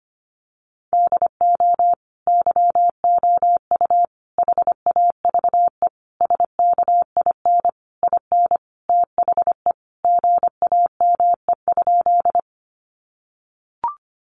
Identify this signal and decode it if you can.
Morse code, heard as a keyed tone, about 25 wpm, 700 Hz. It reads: DO YOU HAVE SKIN IN THE GAME?
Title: Sentences from Top 1300 Words 25wpm